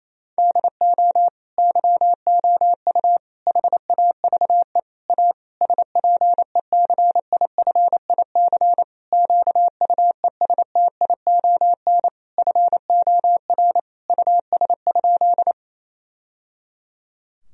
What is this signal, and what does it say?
Morse code, heard as a keyed tone, about 28 wpm, 700 Hz. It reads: DO YOU HAVE A SPECIFIC QUESTION FOR US?
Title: Sentences from Top 1000 Words - Review Part 3 - 28wpm